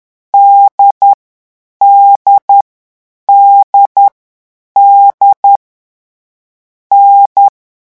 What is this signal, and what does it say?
Morse code, heard as a keyed tone, about 10 wpm, 800 Hz. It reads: D D D D N
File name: morse mp3 / lecon4-vitesse 10